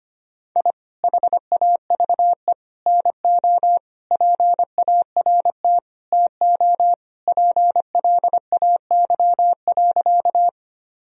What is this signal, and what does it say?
Morse code, heard as a keyed tone, about 25 wpm, 700 Hz. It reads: I HAVE NO PART TO PLAY.
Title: Sentences from Top 1000 Words - Review Part 1 - 25wpm